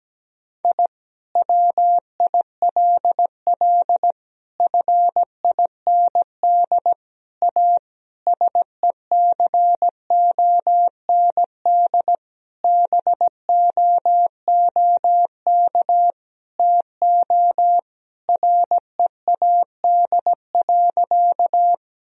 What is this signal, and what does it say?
Morse code, heard as a keyed tone, about 17 wpm, 700 Hz. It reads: I WILL FIND A SECOND BOOK TO READ.
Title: Sentences from Top 1000 Words - Review Part 1 - 17wpm